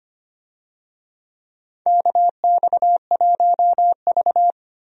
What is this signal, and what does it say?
Morse code, heard as a keyed tone, about 25 wpm, 700 Hz. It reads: KX1V
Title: USA Call Signs - 4 Characters - Repeated 3x - 25wpm